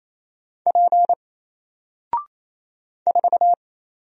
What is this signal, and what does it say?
Morse code, heard as a keyed tone, about 28 wpm, 700 Hz. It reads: P 4
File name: Single-Letter-Number-28wpm